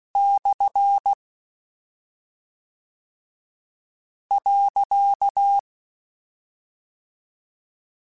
Morse code, heard as a keyed tone, about 16 wpm, 800 Hz. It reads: / .
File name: Misc / Misc-mp3s / Punct @10